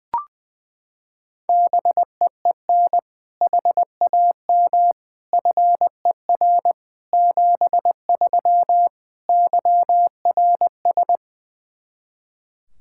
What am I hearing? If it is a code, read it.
Morse code, heard as a keyed tone, about 20 wpm, 700 Hz. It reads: BEEN HAM FER 73 YRS
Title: K1USN SST QSO Practice 20wpm